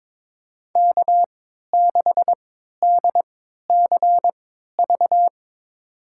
Morse code, heard as a keyed tone, about 22 wpm, spaced as for 13 wpm, 700 Hz. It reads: K6DCV
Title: USA Call Signs - 5 Characters - 3x Character Spacing - 22wpm